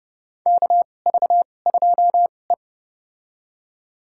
Morse code, heard as a keyed tone, about 32 wpm, spaced as for 22 wpm, 700 Hz. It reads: KV2E
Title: USA Call Signs - 4 Characters - 2x Character Spacing - 30wpm